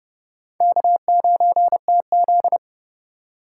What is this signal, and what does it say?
Morse code, heard as a keyed tone, about 28 wpm, 700 Hz. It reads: K9TZ